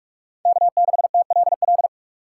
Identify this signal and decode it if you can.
Morse code, heard as a keyed tone, about 58 wpm, 700 Hz. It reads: K6TLL